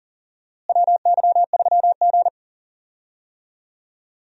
Morse code, heard as a keyed tone, about 40 wpm, 700 Hz. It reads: WY3G
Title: USA Call Signs - 4 Characters - Repeated 3x - 40wpm